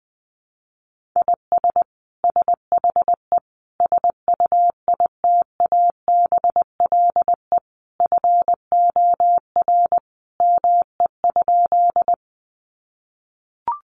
Morse code, heard as a keyed tone, about 20 wpm, 700 Hz. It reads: IS SHE SUITABLE FOR ME?
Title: Sentences from Top 1400 Words 20wpm